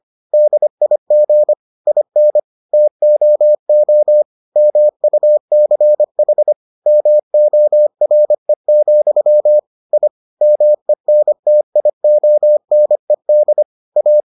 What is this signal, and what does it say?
Morse code, heard as a keyed tone, about 25 wpm, 600 Hz. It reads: DIG IN TOO MUCH MORE, I MENTIONED A